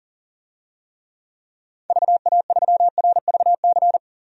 Morse code, heard as a keyed tone, about 40 wpm, 700 Hz. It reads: VA3RVC
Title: DXCC-100 Call Signs - 6 Characters - 40wpm